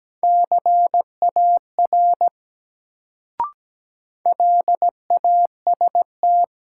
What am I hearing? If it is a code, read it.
Morse code, heard as a keyed tone, about 17 wpm, 700 Hz. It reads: CAR LAST
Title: Sentences from Top 300 Words - New Word Drill - 17wpm